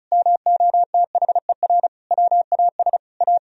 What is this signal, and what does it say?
Morse code, heard as a keyed tone, about 36 wpm, 700 Hz. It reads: MOTHER WAS A
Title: Sentences from Top 1000 Words - Review Part 2 - 35wpm